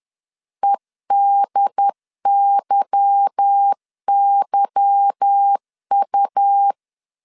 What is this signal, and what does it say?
Morse code, heard as a keyed tone, about 10 wpm, 800 Hz. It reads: EDYYU
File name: LEZIONI / 8 YJ45 / 8 Y 10wpm 2615